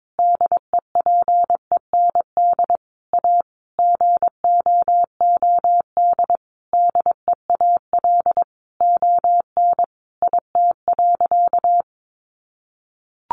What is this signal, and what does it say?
Morse code, heard as a keyed tone, about 22 wpm, 700 Hz. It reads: DEPEND A GOOD DEAL ON IT.
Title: Sentences from Top 1300 Words 22wpm